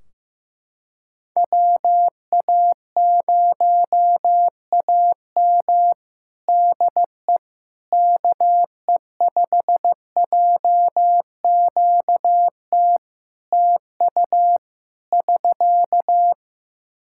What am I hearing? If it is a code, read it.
Morse code, heard as a keyed tone, about 16 wpm, 700 Hz. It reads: WA0AM DE KE5JQT TU <SK>
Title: Practice QSO - Exchange 4 - Ending - 15wpm